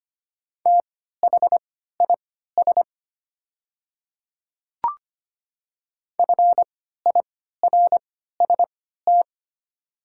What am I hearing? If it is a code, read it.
Morse code, heard as a keyed tone, about 25 wpm, spaced as for 14 wpm, 700 Hz. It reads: THIS FIRST
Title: 3x Character Spacing - Top 100 Words - 25wpm